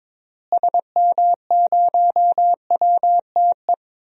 Morse code, heard as a keyed tone, about 22 wpm, 700 Hz. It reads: SM0WTE
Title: DXCC-100 Call Signs - 6 Characters - Repeated 2x - 22wpm